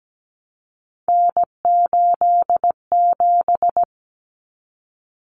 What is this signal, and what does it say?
Morse code, heard as a keyed tone, about 17 wpm, 700 Hz. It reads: N87
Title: Common Cut Numbers - 3-Digit 17wpm